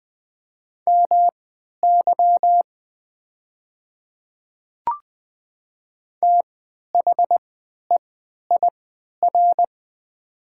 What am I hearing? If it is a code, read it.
Morse code, heard as a keyed tone, about 20 wpm, spaced as for 11 wpm, 700 Hz. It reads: MY THEIR
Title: 3x Character Spacing - Top 100 Words - 20wpm